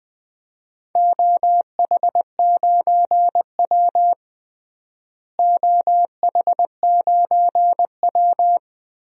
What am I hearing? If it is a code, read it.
Morse code, heard as a keyed tone, about 20 wpm, 700 Hz. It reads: OH9W OH9W
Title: DXCC-100 Call Signs - 4 Characters - Repeated 2x - 20wpm